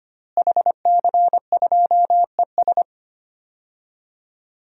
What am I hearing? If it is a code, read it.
Morse code, heard as a keyed tone, about 25 wpm, 700 Hz. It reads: HC2ES
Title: DXCC-100 Call Signs - 5 Characters - Repeated 3x - 25wpm